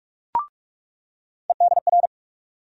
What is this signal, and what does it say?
Morse code, heard as a keyed tone, about 43 wpm, 700 Hz. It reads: EDR
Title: USA Call Sign Postfixes - 3 Letters 45wpm